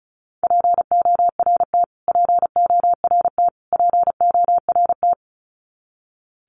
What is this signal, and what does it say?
Morse code, heard as a keyed tone, about 34 wpm, 700 Hz. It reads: PORT PORT PORT
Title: Top 200 Words Repeated 35wpm